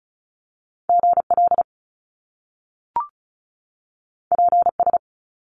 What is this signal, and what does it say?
Morse code, heard as a keyed tone, about 35 wpm, 700 Hz. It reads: GL PS
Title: Top Two Letter Combinations in Words 35wpm